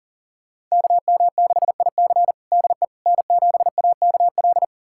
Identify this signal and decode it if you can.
Morse code, heard as a keyed tone, about 40 wpm, 700 Hz. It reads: KM6IC DE N7AKL
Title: USA Callsign DE Callsign - 5 Characters - 40wpm